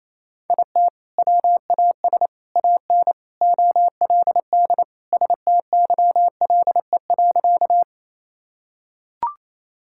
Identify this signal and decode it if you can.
Morse code, heard as a keyed tone, about 28 wpm, 700 Hz. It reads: IT WAS AN OLD STYLE.
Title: Sentences from Top 1000 Words - Review Part 5 - 28wpm